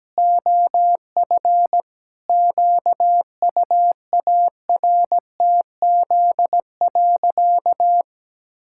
Morse code, heard as a keyed tone, about 17 wpm, 700 Hz. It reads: OF QUARTZ.
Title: Pangram Challenge v2 - 17wpm